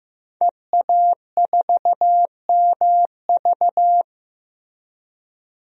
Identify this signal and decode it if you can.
Morse code, heard as a keyed tone, about 15 wpm, 700 Hz. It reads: EA4MV
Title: DXCC-100 Call Signs - 5 Characters - Repeated 3x - 15wpm